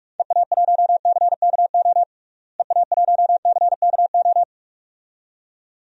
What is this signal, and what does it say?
Morse code, heard as a keyed tone, about 46 wpm, 700 Hz. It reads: EA1CKO EA1CKO
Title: DXCC-100 Call Signs - 6 Characters - Repeated 2x - 45wpm